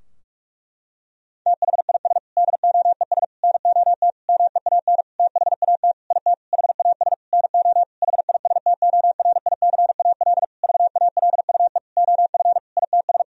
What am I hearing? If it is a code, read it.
Morse code, heard as a keyed tone, about 46 wpm, 700 Hz. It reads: THIS DOES NOT MEAN THAT IT HAS NO HISTORICAL VALUE OF ITS
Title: Sentences from Top 1000 Words - Review Part 4 - 45wpm